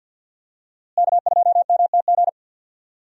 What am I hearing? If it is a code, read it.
Morse code, heard as a keyed tone, about 52 wpm, 700 Hz. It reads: K2MTG
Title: USA Call Signs - 5 Characters - Repeated 3x - 50wpm